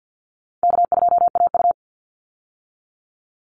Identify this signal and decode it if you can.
Morse code, heard as a keyed tone, about 50 wpm, 700 Hz. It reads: K2AU
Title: Call Signs - 50wpm